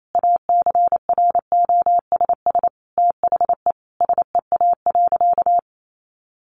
Morse code, heard as a keyed tone, about 28 wpm, 700 Hz. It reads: ACROSS THE SEA.
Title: Sentences from Top 1300 Words 28wpm